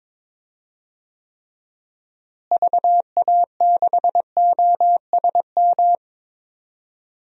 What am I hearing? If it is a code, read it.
Morse code, heard as a keyed tone, about 22 wpm, 700 Hz. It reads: VA6OSM
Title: DXCC-100 Call Signs - 6 Characters - Repeated 2x - 22wpm